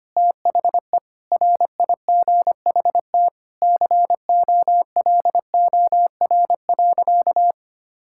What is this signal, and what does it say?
Morse code, heard as a keyed tone, about 25 wpm, 700 Hz. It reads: THE RIGHT COLOR.
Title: Sentences from Top 1000 Words - Review Part 6 - 25wpm